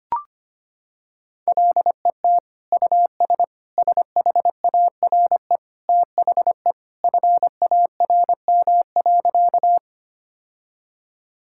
Morse code, heard as a keyed tone, about 25 wpm, 700 Hz. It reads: LET US SHARE THE FARM.
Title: Sentences from Top 1000 Words - Review Part 6 - 25wpm